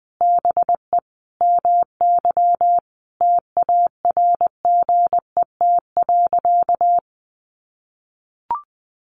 Morse code, heard as a keyed tone, about 20 wpm, 700 Hz. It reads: BE MY TARGET.
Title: Sentences from Top 1200 Words 20wpm